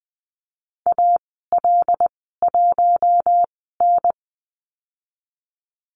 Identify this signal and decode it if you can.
Morse code, heard as a keyed tone, about 20 wpm, spaced as for 15 wpm, 700 Hz. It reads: AL1N
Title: 088 Lesson 15g - Number 5 - Call Signs 20wpm